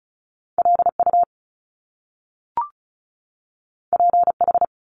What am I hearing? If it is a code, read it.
Morse code, heard as a keyed tone, about 36 wpm, 700 Hz. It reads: LU PH
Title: Top Two Letter Combinations in Words 35wpm